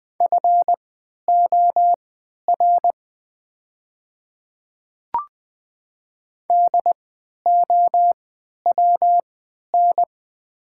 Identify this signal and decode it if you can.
Morse code, heard as a keyed tone, about 20 wpm, spaced as for 11 wpm, 700 Hz. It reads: FOR DOWN
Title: 3x Character Spacing - Top 300 Words - 20wpm